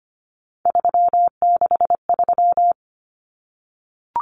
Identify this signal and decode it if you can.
Morse code, heard as a keyed tone, about 25 wpm, 700 Hz. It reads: 363
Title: Common Cut Numbers - 3-Digit 25wpm